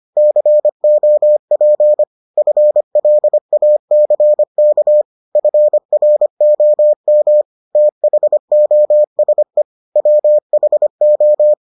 Morse code, heard as a keyed tone, about 25 wpm, 600 Hz. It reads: COP FLACK FROM THOSE WHO